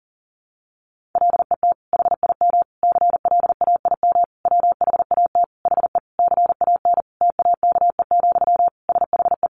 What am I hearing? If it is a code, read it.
Morse code, heard as a keyed tone, about 40 wpm, 700 Hz. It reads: LET HIM CLAIM WHAT HE CAN TAKE, SHE
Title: Sentences from Top 1500 Words 40wpm